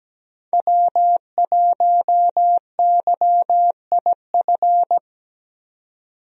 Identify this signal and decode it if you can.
Morse code, heard as a keyed tone, about 17 wpm, 700 Hz. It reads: W1YIF
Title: USA Call Signs - 5 Characters - 17wpm